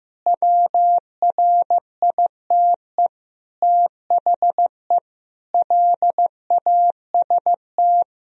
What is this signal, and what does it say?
Morse code, heard as a keyed tone, about 15 wpm, 700 Hz. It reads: WRITE THE LAST
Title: Sentences from Top 1000 Words - Review Part 1 - 15wpm